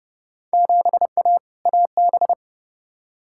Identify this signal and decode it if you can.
Morse code, heard as a keyed tone, about 29 wpm, 700 Hz. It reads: 7A AB
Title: Field Day Exchanges Encore - 30wpm